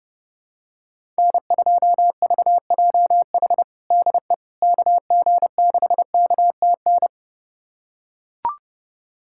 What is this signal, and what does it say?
Morse code, heard as a keyed tone, about 28 wpm, 700 Hz. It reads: N2VJH DE KG6KTN
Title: USA Callsign DE Callsign 30wpm